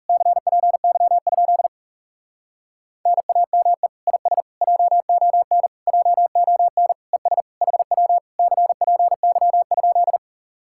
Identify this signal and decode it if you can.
Morse code, heard as a keyed tone, about 40 wpm, 700 Hz. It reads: CPY? NAME IS JON JON ES HW CPY?